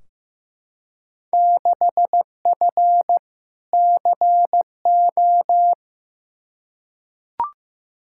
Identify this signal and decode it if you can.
Morse code, heard as a keyed tone, about 15 wpm, 700 Hz. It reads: 6F CO
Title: Field Day Exchanges Encore - 15wpm